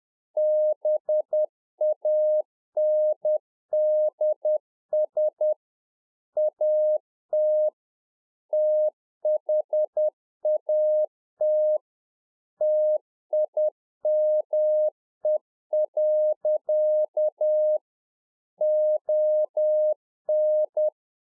Morse code, heard as a keyed tone, about 10 wpm, 610 Hz. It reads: BANDS AT THAT TIME. ON
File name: 2011-04-mp3 / UNFOGETTABLE QSO 10WPM